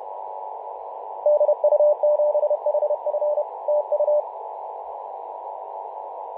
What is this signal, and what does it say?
Morse code, heard as a keyed tone, about 31 wpm, 600 Hz, with radio noise. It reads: DU7HF TU
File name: Mar/20 2349z FW5JJ 28.011MHz CW